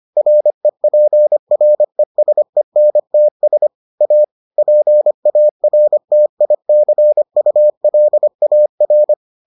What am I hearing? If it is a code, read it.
Morse code, heard as a keyed tone, about 25 wpm, 600 Hz. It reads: REPRESENTS A PARTICULAR